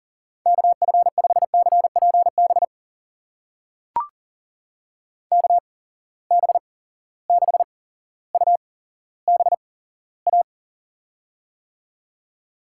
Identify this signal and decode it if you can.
Morse code, heard as a keyed tone, about 40 wpm, spaced as for 17 wpm, 700 Hz. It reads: KF5CPB KB6UBA
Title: USA Call Signs - 6 Characters - 8x Character Spacing - 40wpm